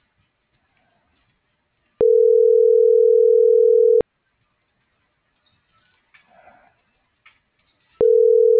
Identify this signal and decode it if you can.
Ringback tone (North America), 440 + 480 Hz.